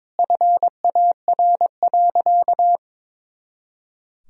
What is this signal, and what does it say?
Morse code, heard as a keyed tone, about 22 wpm, 700 Hz. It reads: FAR.
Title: Sentences from Top 400 Words and Abbreviations 22wpm